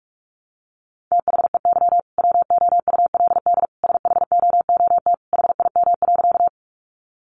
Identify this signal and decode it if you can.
Morse code, heard as a keyed tone, about 44 wpm, 700 Hz. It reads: THEY WOULD SHOOT HIM.
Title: Sentences from Top 1200 Words 45wpm